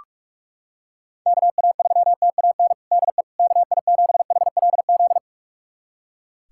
Morse code, heard as a keyed tone, about 45 wpm, 700 Hz. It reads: KA3TAN DE KI7HLZ
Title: USA Callsign DE Callsign - 6 Characters - 45wpm